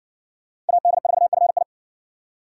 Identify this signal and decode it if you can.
Morse code, heard as a keyed tone, about 60 wpm, 700 Hz. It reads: AD4FI